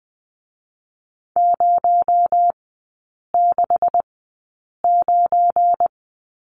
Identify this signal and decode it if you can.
Morse code, heard as a keyed tone, about 20 wpm, 700 Hz. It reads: 0 6 9
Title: 218 Lesson 35e - Number 0 - Three Characters 20wpm